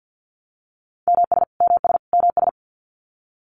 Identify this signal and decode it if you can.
Morse code, heard as a keyed tone, about 49 wpm, 700 Hz. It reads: MS MS MS
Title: US States Repeated 3x 50wpm